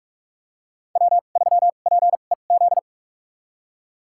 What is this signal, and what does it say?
Morse code, heard as a keyed tone, about 45 wpm, spaced as for 32 wpm, 700 Hz. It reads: W3PEZ
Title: USA Call Signs - 5 Characters - 2x Character Spacing - 45wpm